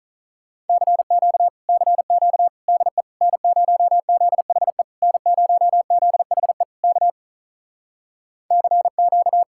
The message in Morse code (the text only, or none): CQ CQ DE N0ZHE N0ZHE K CQ